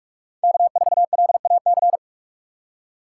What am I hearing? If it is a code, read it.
Morse code, heard as a keyed tone, about 44 wpm, 700 Hz. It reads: K4LAC